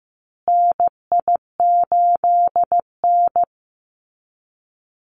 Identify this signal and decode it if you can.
Morse code, heard as a keyed tone, about 14 wpm, 700 Hz. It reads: NI8N
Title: Call Signs Repeated 2x - 15wpm